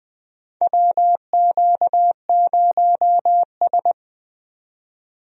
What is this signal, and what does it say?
Morse code, heard as a keyed tone, about 20 wpm, 700 Hz. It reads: WQ0S
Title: USA Call Signs - 4 Characters - 20wpm